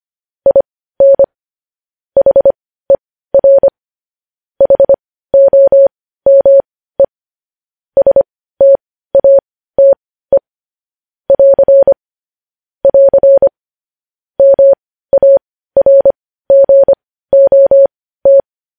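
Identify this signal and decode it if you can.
Morse code, heard as a keyed tone, about 25 wpm, spaced as for 15 wpm, 550 Hz. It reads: IN HER HOME STATE <AR> <AR> MARGOT
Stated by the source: Morse news headlines practice at 15.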